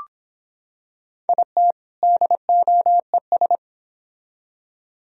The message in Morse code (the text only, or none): IT DOES